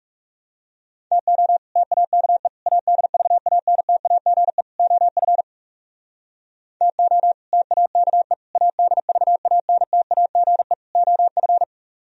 Morse code, heard as a keyed tone, about 40 wpm, 700 Hz. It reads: TO TAKE ADVANTAGE OF TO TAKE ADVANTAGE OF